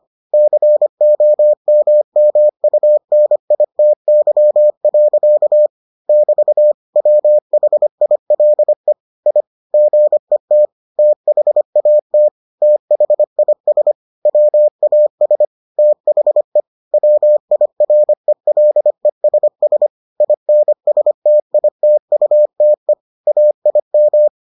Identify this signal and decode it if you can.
Morse code, heard as a keyed tone, about 25 wpm, 600 Hz. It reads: COMMUNITY. <BT> WHILE I GET THAT THIS WAS THE WIRELESS INSTITUTE AIM